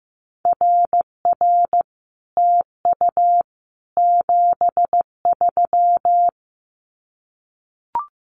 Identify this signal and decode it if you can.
Morse code, heard as a keyed tone, about 15 wpm, 700 Hz. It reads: RR TU 73
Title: Common QSO Phrases Encore 15wpm